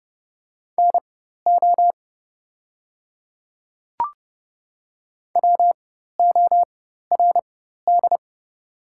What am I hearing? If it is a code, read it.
Morse code, heard as a keyed tone, about 30 wpm, spaced as for 14 wpm, 690 Hz. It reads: NO WORD
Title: 4x Character Spacing - Top 100 Words - 30wpm